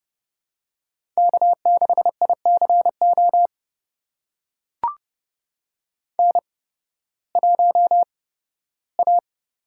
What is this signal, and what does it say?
Morse code, heard as a keyed tone, about 32 wpm, spaced as for 16 wpm, 700 Hz. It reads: K6ICO N1A
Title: USA Call Signs - 5 Characters - 8x Character Spacing - 30wpm